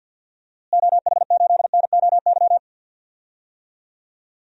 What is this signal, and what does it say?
Morse code, heard as a keyed tone, about 50 wpm, 700 Hz. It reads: OH8NOY
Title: DXCC-100 Call Signs - 6 Characters - Repeated 3x - 50wpm